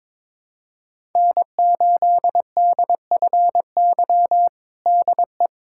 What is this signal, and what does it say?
Morse code, heard as a keyed tone, about 22 wpm, 700 Hz. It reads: N8DFY DE
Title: USA Callsign DE Callsign - 5 Characters - 22wpm